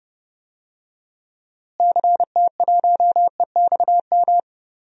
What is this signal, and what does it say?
Morse code, heard as a keyed tone, about 32 wpm, 700 Hz. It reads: CT1EXM
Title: DXCC-100 Call Signs - 6 Characters - Repeated 2x - 30wpm